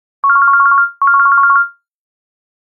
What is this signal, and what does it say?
There is no code.